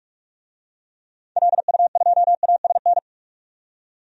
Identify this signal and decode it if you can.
Morse code, heard as a keyed tone, about 46 wpm, 700 Hz. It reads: LU2ASN
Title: DXCC-100 Call Signs - 6 Characters - 45wpm